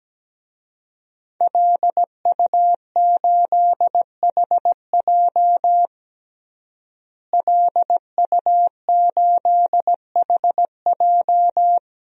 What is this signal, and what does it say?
Morse code, heard as a keyed tone, about 17 wpm, 700 Hz. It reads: LU8HJ LU8HJ
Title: DXCC-100 Call Signs - 5 Characters - Repeated 3x - 17wpm